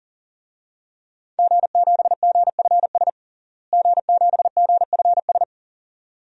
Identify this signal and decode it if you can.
Morse code, heard as a keyed tone, about 40 wpm, 700 Hz. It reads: G7GFS G7GFS
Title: DXCC-100 Call Signs - 5 Characters - Repeated 2x - 40wpm